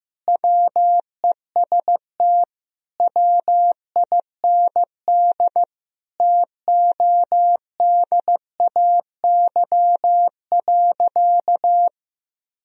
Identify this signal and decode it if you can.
Morse code, heard as a keyed tone, about 15 wpm, 700 Hz. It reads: WEST WIND TODAY.
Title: Sentences from Top 1000 Words - Review Part 2 - 15wpm